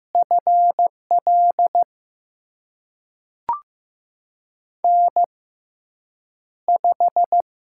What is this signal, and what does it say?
Morse code, heard as a keyed tone, about 15 wpm, spaced as for 8 wpm, 700 Hz. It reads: FL N5
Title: USA Call Signs - 5 Characters - 6x Character Spacing - 15wpm